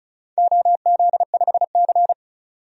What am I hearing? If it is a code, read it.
Morse code, heard as a keyed tone, about 36 wpm, 700 Hz. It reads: OZ5C
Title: DXCC-100 Call Signs - 4 Characters - 35wpm